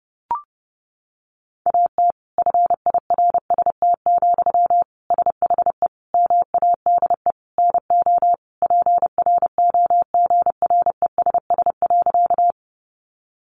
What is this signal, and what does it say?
Morse code, heard as a keyed tone, about 30 wpm, 700 Hz. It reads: AT FIRST, SHE MADE NO PROGRESS.
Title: Sentences from Top 1400 Words 30wpm